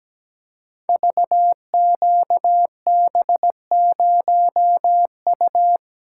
Morse code, heard as a keyed tone, about 17 wpm, 700 Hz. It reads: VQB0U
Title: Code Groups - 5 Letter:Numbers - 17wpm